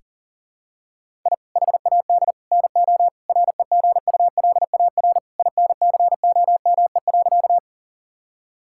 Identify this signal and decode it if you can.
Morse code, heard as a keyed tone, about 40 wpm, 700 Hz. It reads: I HAD NO REGULAR INCOME.
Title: Sentences from Top 1000 Words - Review Part 6 - 40wpm